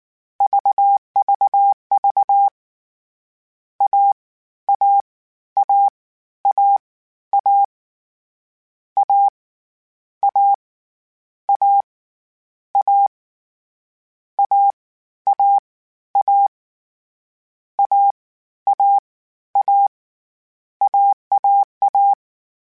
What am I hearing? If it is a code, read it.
Morse code, heard as a keyed tone, about 19 wpm, spaced as for 13 wpm, 800 Hz. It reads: VVV AAAAA A A A A AAA AAA AAA
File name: Difficult Combos / Difficult Combos-mp3s / AWJ1 @18